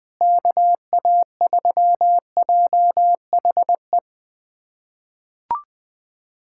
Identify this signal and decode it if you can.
Morse code, heard as a keyed tone, about 21 wpm, 700 Hz. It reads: KA3JHE
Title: USA Callsign DE Callsign - 6 Characters - 20wpm